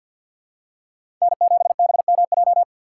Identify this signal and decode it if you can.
Morse code, heard as a keyed tone, about 47 wpm, 700 Hz. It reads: N7BMJ
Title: USA Call Signs - 5 Characters - 4x Character Spacing - 50wpm